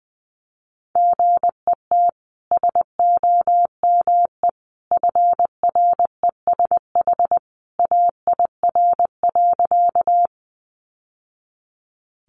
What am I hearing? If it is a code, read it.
Morse code, heard as a keyed tone, about 20 wpm, 700 Hz. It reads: GET SOME FRESH AIR.
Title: Sentences from Top 1200 Words 20wpm